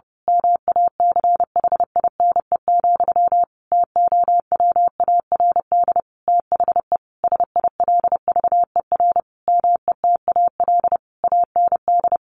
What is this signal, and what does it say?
Morse code, heard as a keyed tone, about 30 wpm, 700 Hz. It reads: MACHINE, TOWARD THE SILVER METAL AND
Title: A Sound Of Thunder 30wpm